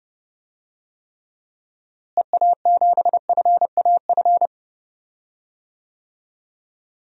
Morse code, heard as a keyed tone, about 30 wpm, 700 Hz. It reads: EA7FAF
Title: DXCC-100 Call Signs - 6 Characters - Repeated 3x - 30wpm